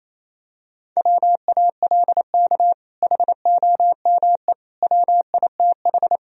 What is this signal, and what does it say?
Morse code, heard as a keyed tone, about 27 wpm, 700 Hz. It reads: WALK HOME WITH
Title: Sentences from Top 1000 Words - Review Part 1 - 28wpm